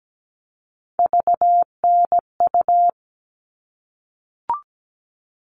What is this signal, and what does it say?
Morse code, heard as a keyed tone, about 17 wpm, 700 Hz. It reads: VNU
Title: All Cut Numbers - 3-Digit 17wpm